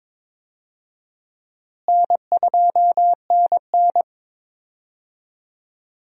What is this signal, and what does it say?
Morse code, heard as a keyed tone, about 22 wpm, 700 Hz. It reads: N2NN
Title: USA Call Signs - 4 Characters - Repeated 3x - 22wpm